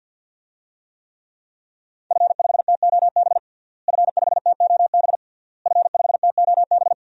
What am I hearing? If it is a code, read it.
Morse code, heard as a keyed tone, about 51 wpm, 700 Hz. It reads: F5TOB F5TOB F5TOB
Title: DXCC-100 Call Signs - 5 Characters - Repeated 3x - 50wpm